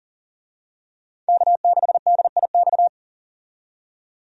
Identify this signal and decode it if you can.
Morse code, heard as a keyed tone, about 40 wpm, 700 Hz. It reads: K6DIX
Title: USA Call Signs - 5 Characters - 40wpm